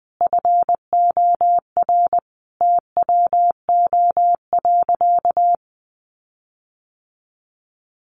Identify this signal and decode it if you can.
Morse code, heard as a keyed tone, about 20 wpm, 700 Hz. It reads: FOR TWO.
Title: Sentences from Top 1400 Words 20wpm